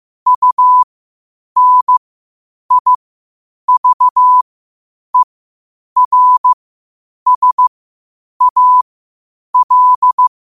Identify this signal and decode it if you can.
Morse code, heard as a keyed tone, about 15 wpm, spaced as for 8 wpm, 1000 Hz. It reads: UNIVERSAL
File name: qotdfiles / 8wpm